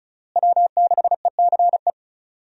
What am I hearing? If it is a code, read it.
Morse code, heard as a keyed tone, about 36 wpm, 700 Hz. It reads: W6ECE
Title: USA Call Signs - 5 Characters - Repeated 2x - 35wpm